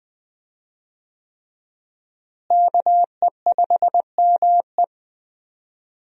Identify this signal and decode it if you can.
Morse code, heard as a keyed tone, about 20 wpm, 700 Hz. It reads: KE5ME
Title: USA Call Signs - 5 Characters - Repeated 3x - 20wpm